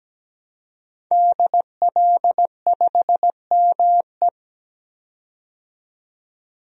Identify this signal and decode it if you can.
Morse code, heard as a keyed tone, about 17 wpm, 700 Hz. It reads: DL5ME
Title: DXCC-100 Call Signs - 5 Characters - 17wpm